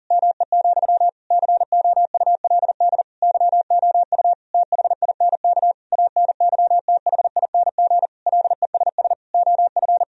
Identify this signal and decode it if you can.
Morse code, heard as a keyed tone, about 40 wpm, 700 Hz. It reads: ME, COULD YOU THINK ANYTHING LESS OF
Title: Sentences from Top 1000 Words - Review Part 2 - 40wpm